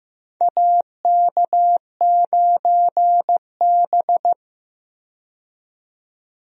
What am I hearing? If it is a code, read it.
Morse code, heard as a keyed tone, about 15 wpm, 700 Hz. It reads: AK9B
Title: USA Call Signs - 4 Characters - 15wpm